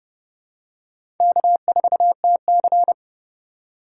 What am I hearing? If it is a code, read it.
Morse code, heard as a keyed tone, about 32 wpm, 700 Hz. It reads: K4TC
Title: USA Call Signs - 4 Characters - 4x Character Spacing - 30wpm